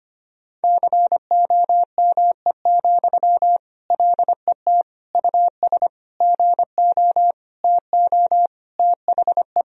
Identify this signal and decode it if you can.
Morse code, heard as a keyed tone, about 25 wpm, 700 Hz. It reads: COME, LET US GO TO THE
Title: Sentences from Top 1000 Words - Review Part 4 - 25wpm